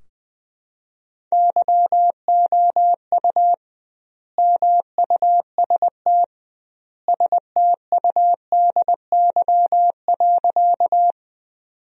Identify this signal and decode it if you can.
Morse code, heard as a keyed tone, about 21 wpm, 700 Hz. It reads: YOU MUST STUDY.
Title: 2x Word-Spacing - Sentences from Top 300 Words - 20wpm